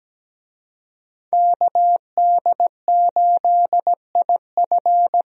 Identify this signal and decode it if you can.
Morse code, heard as a keyed tone, about 17 wpm, 700 Hz. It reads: KD8IF
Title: USA Call Signs - 5 Characters - Repeated 2x - 17wpm